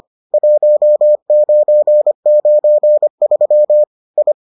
Morse code, heard as a keyed tone, about 25 wpm, 600 Hz. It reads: 1993 I